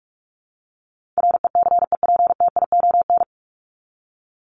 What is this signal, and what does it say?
Morse code, heard as a keyed tone, about 44 wpm, 700 Hz. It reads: RECEPTION
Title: Top 1751-2000 Words 45wpm